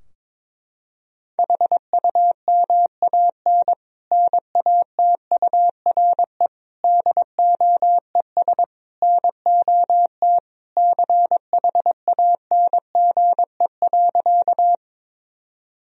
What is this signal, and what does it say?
Morse code, heard as a keyed tone, about 22 wpm, 700 Hz. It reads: HUMAN NATURE DOES NOT CHANGE.
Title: Sentences from Top 1000 Words - Review Part 3 - 22wpm